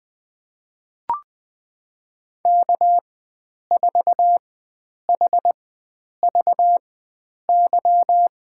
Morse code, heard as a keyed tone, about 21 wpm, spaced as for 9 wpm, 700 Hz. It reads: K4HVY
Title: USA Call Signs - 5 Characters - 4x Character Spacing - 20wpm